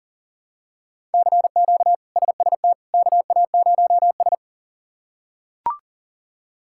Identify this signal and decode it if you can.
Morse code, heard as a keyed tone, about 37 wpm, 700 Hz. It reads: CQ SST KA0S
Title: K1USN SST QSO Practice - 40wpm